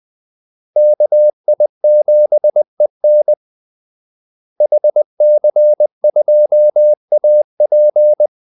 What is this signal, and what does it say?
Morse code, heard as a keyed tone, about 20 wpm, 600 Hz. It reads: KI7EN HC2AP
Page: Kaldesignaler 20 wpm